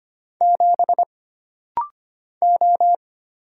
Morse code, heard as a keyed tone, about 25 wpm, 700 Hz. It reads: 7 O
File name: Single Character (Letter:Number:Punctuation:Prosign) - Rapid-Fire - 25wpm